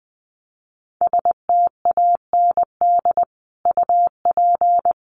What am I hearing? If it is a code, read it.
Morse code, heard as a keyed tone, about 20 wpm, 700 Hz. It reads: STAND UP
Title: Sentences from Top 1000 Words 20wpm